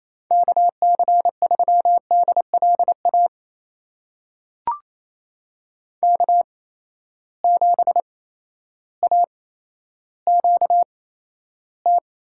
Morse code, heard as a keyed tone, about 28 wpm, spaced as for 13 wpm, 700 Hz. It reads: KC3DLA K7AQT
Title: USA Call Signs - 8x Character Spacing - 28wpm